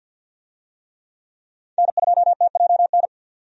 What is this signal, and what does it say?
Morse code, heard as a keyed tone, about 52 wpm, 700 Hz. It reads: N2TJN
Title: USA Call Signs - 5 Characters - 50wpm